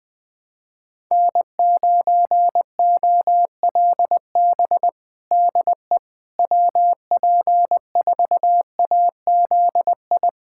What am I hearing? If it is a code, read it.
Morse code, heard as a keyed tone, about 21 wpm, 700 Hz. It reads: N9OLB DE WP4AZI
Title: USA Callsign DE Callsign 20wpm